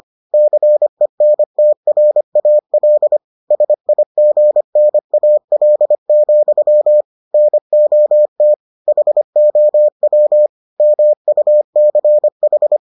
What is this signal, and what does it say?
Morse code, heard as a keyed tone, about 25 wpm, 600 Hz. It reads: CENTRAL SIGNAL, NOT HOW MUCH